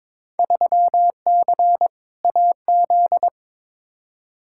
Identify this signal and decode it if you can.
Morse code, heard as a keyed tone, about 22 wpm, 700 Hz. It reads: 3C AZ
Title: Field Day Exchanges Encore - 22wpm